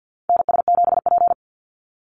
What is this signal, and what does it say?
Morse code, heard as a keyed tone, about 49 wpm, 700 Hz. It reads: NS7P
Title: NA SOTA Chaser Callsigns 50wpm